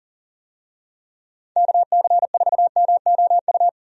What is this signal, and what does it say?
Morse code, heard as a keyed tone, about 40 wpm, 700 Hz. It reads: KC4MOU
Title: USA Call Signs - 6 Characters - 40wpm